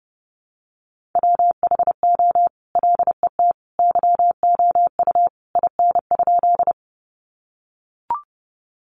Morse code, heard as a keyed tone, about 30 wpm, 700 Hz. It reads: WHO LET YOU IN?
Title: Sentences from Top 300 Words 30wpm